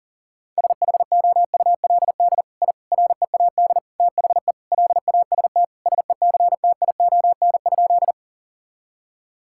Morse code, heard as a keyed tone, about 40 wpm, 700 Hz. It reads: SHOULD I READ THE LAST SECTION?
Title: Sentences from Top 1000 Words - Review Part 5 - 40wpm